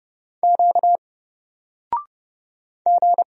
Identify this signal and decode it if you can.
Morse code, heard as a keyed tone, about 30 wpm, 700 Hz. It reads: Q G
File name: Single-Letter-Number-30wpm